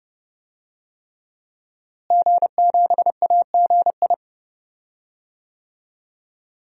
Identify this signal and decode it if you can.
Morse code, heard as a keyed tone, about 31 wpm, 700 Hz. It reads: G7AGI
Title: DXCC-100 Call Signs - 5 Characters - Repeated 3x - 30wpm